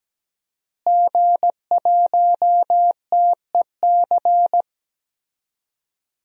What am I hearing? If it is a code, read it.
Morse code, heard as a keyed tone, about 17 wpm, 700 Hz. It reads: G1TEC
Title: DXCC-100 Call Signs - 5 Characters - Repeated 3x - 17wpm